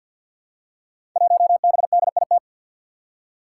Morse code, heard as a keyed tone, about 46 wpm, 700 Hz. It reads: 1BDIT
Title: Five Challenging Letter-Number Combinations 50wpm